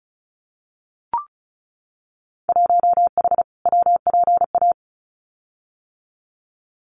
Morse code, heard as a keyed tone, about 34 wpm, 700 Hz. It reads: 1H WPA
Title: Winter Field Day Exchanges 35wpm